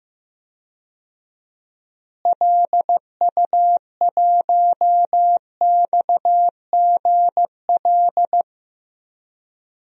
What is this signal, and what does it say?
Morse code, heard as a keyed tone, about 16 wpm, 700 Hz. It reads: LU1XGL
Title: DXCC-100 Call Signs - 6 Characters - Repeated 3x - 15wpm